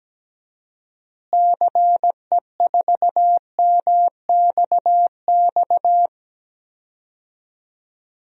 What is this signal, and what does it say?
Morse code, heard as a keyed tone, about 17 wpm, 700 Hz. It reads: CE4MXX
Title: DXCC-100 Call Signs - 6 Characters - Repeated 2x - 17wpm